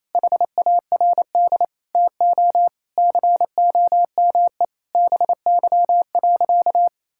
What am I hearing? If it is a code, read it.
Morse code, heard as a keyed tone, about 28 wpm, 700 Hz. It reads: HARD TO COME BY.
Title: Sentences from Top 1000 Words - Review Part 5 - 28wpm